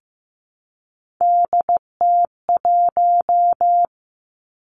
Morse code, heard as a keyed tone, about 15 wpm, 700 Hz. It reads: DT1
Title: Three Challenging Letter-Number Combinations 15wpm